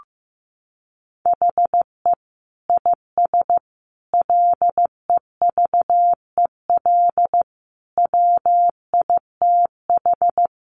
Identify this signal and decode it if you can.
Morse code, heard as a keyed tone, about 15 wpm, 700 Hz. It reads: HE IS LEVEL WITH
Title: Sentences from Top 700 Words 15wpm